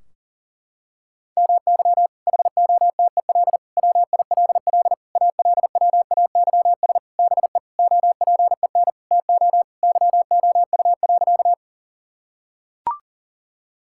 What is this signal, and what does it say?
Morse code, heard as a keyed tone, about 40 wpm, 700 Hz. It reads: MY HOTEL WILL ALWAYS BE OPEN TO YOU.
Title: Sentences from Top 1600 Words 40wpm